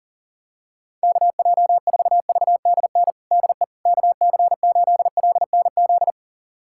Morse code, heard as a keyed tone, about 40 wpm, 700 Hz. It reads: KJ4VDN DE KC8LNZ
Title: USA Callsign DE Callsign - 6 Characters - 40wpm